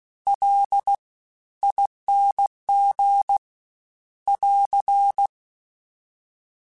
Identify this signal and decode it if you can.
Morse code, heard as a keyed tone, about 16 wpm, spaced as for 13 wpm, 800 Hz. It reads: LING <AR>
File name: Special INGs-mp3s / Special INGs-mp3s / Fast ing @15